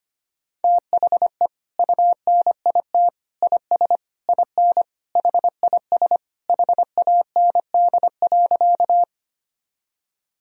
Morse code, heard as a keyed tone, about 25 wpm, 700 Hz. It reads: THE UNIT IS IN HIS HAND.
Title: Sentences from Top 1000 Words - Review Part 2 - 25wpm